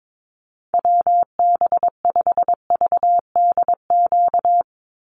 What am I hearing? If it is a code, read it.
Morse code, heard as a keyed tone, about 22 wpm, 700 Hz. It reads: WB5VDQ
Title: USA Call Signs 22wpm